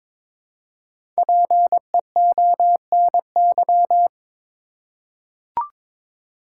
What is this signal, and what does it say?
Morse code, heard as a keyed tone, about 22 wpm, 700 Hz. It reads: PEONY
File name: Difficult Words with Hints - 22wpm